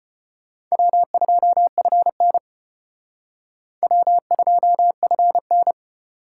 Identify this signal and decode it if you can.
Morse code, heard as a keyed tone, about 30 wpm, 700 Hz. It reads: W2FN W2FN
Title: Speed Racing - USA Call Signs 25wpm